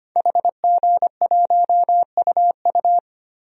Morse code, heard as a keyed tone, about 25 wpm, 700 Hz. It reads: HG1UU
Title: DXCC-100 Call Signs - 5 Characters - 25wpm